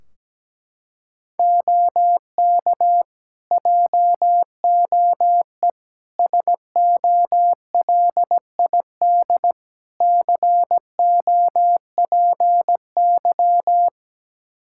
Morse code, heard as a keyed tone, about 17 wpm, 700 Hz. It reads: OK JOE SOLID COPY
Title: Practice QSO - Exchange 3 - More Info - 17wpm